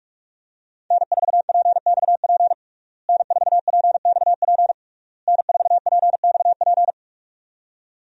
Morse code, heard as a keyed tone, about 42 wpm, 700 Hz. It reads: N4PXP N4PXP N4PXP
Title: USA Call Signs - 5 Characters - Repeated 3x - 45wpm